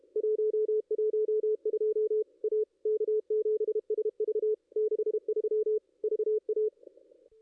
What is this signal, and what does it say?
Morse code, heard as a keyed tone, about 30 wpm, 430 Hz. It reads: 112 A K7SV 63 VA
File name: contests / ss-cw-k5tr-op-n5rz -2200